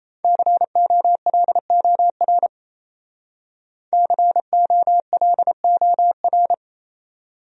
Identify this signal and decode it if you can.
Morse code, heard as a keyed tone, about 35 wpm, 690 Hz. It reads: COLOR COLOR
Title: Speed-Racing Top 500 Words 28wpm